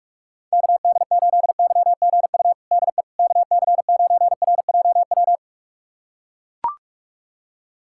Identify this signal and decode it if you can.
Morse code, heard as a keyed tone, about 41 wpm, 700 Hz. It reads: KD8YGU DE KC9RJW
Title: USA Callsign DE Callsign - 6 Characters - 45wpm